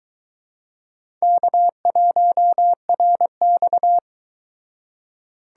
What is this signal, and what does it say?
Morse code, heard as a keyed tone, about 23 wpm, 700 Hz. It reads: K1RX